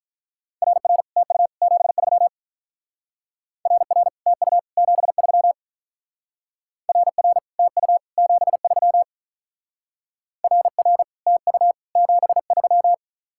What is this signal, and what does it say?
Morse code, heard as a keyed tone, about 43 wpm, 700 Hz. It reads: RR TU 73 RR TU 73 RR TU 73 RR TU 73
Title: Speed-Racing Common QSO Phrases Encore - 35wpm